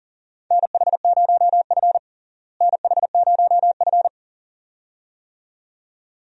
Morse code, heard as a keyed tone, about 40 wpm, 700 Hz. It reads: NH0F NH0F
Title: USA Call Signs - 4 Characters - Repeated 2x - 40wpm